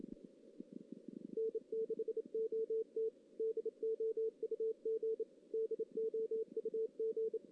Morse code, heard as a keyed tone, about 30 wpm, 440 Hz. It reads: N6OT DOUG DOUG
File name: 2016 JAN CW NAQP / 2016 Jan naqp k5ot - 0-3